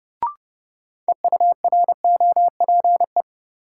Morse code, heard as a keyed tone, about 30 wpm, 700 Hz. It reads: EUROPE
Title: Sets of 5 Words Encore - New Word Drill - 30wpm